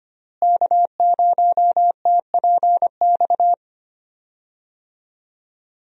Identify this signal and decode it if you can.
Morse code, heard as a keyed tone, about 25 wpm, 700 Hz. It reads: K0TPX
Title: USA Call Signs - 5 Characters - Repeated 2x - 25wpm